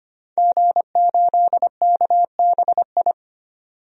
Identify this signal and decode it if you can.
Morse code, heard as a keyed tone, about 25 wpm, 700 Hz. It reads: G8KBI
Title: DXCC-100 Call Signs - 5 Characters - 25wpm